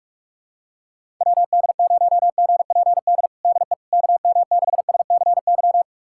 Morse code, heard as a keyed tone, about 46 wpm, 700 Hz. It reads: WD0GPD DE KM6SCY
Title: USA Callsign DE Callsign 45wpm